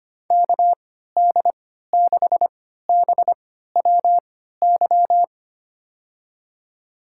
Morse code, heard as a keyed tone, about 25 wpm, spaced as for 14 wpm, 690 Hz. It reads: KD6BWY
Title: USA Call Signs - 6 Characters - 3x Character Spacing - 25wpm